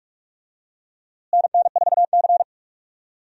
Morse code, heard as a keyed tone, about 46 wpm, 700 Hz. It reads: NN4C
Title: USA Call Signs - 4 Characters - Repeated 3x - 45wpm